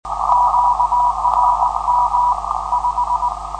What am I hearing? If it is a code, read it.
Morse code, heard as a keyed tone, about 19 wpm, 980 Hz, with noise. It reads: QRM 5
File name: 472khz qso web / 472khz 14dec2012 / dk7fc-4 qrm5